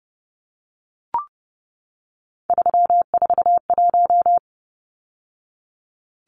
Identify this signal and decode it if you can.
Morse code, heard as a keyed tone, about 30 wpm, 700 Hz. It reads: 341
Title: Numbers - 3-Digit 30wpm